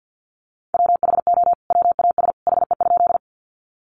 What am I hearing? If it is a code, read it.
Morse code, heard as a keyed tone, about 50 wpm, 700 Hz. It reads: WHO WAS HE?
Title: Sentences from Top 100 Words 50wpm